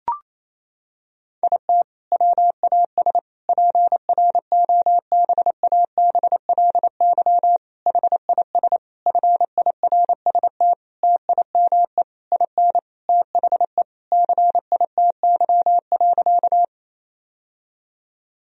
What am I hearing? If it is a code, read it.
Morse code, heard as a keyed tone, about 28 wpm, 700 Hz. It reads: IT WAS PROBABLY HIS FIRST TIME IN THE CITY.
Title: Sentences from Top 1000 Words - Review Part 4 - 28wpm